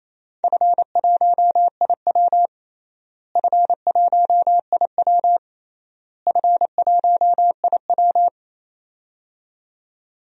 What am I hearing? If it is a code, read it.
Morse code, heard as a keyed tone, about 27 wpm, 700 Hz. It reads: F1IW F1IW F1IW
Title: DXCC-100 Call Signs - 4 Characters - Repeated 3x - 28wpm